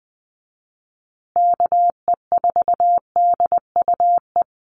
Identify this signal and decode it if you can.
Morse code, heard as a keyed tone, about 20 wpm, 700 Hz. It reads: KE4DUE
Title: USA Call Signs Repeated 2x 20wpm